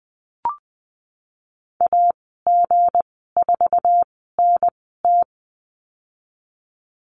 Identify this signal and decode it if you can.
Morse code, heard as a keyed tone, about 20 wpm, spaced as for 15 wpm, 700 Hz. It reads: AG4NT
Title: 167 Lesson 27g - Number 7 - Call Signs 20wpm